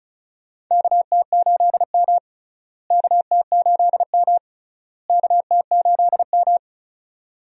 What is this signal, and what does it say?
Morse code, heard as a keyed tone, about 34 wpm, 700 Hz. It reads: KT8M KT8M KT8M
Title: USA Call Signs - 4 Characters - Repeated 3x - 35wpm